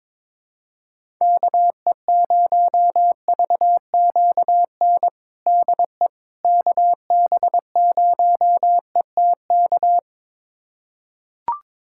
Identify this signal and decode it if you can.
Morse code, heard as a keyed tone, about 22 wpm, 700 Hz. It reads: KE0VQN DE KB0ETK
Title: USA Callsign DE Callsign 22wpm